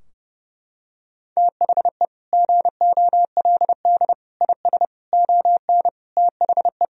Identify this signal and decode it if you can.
Morse code, heard as a keyed tone, about 30 wpm, 700 Hz. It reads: THE GOLD IS ON THE
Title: Sentences from Top 1000 Words - Review Part 3 - 30wpm